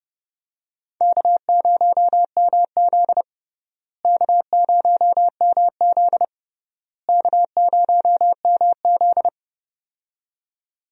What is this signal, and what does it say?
Morse code, heard as a keyed tone, about 32 wpm, 700 Hz. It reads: K0MZ K0MZ K0MZ
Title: USA Call Signs - 4 Characters - Repeated 3x - 30wpm